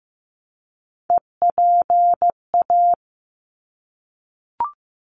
Morse code, heard as a keyed tone, about 15 wpm, 700 Hz. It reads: EPA
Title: ARRL Sections 15wpm